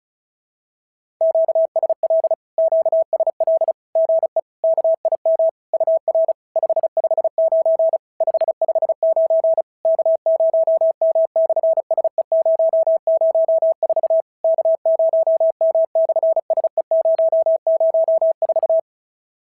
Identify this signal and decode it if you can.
Morse code, heard as a keyed tone, about 36 wpm, 660 Hz. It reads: QSL QSL GE KIM UR 559 559 K0M/SE004 K0M/SE004
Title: SOTA QSO Practice 35wpm